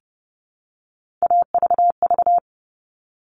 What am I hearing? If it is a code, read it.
Morse code, heard as a keyed tone, about 30 wpm, 700 Hz. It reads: AVV
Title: All Cut Numbers - 3-Digit 30wpm